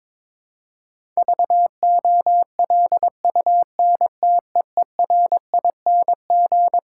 Morse code, heard as a keyed tone, about 21 wpm, 700 Hz. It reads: VOLUNTEERING
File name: Difficult Sentences with Hints - 22wpm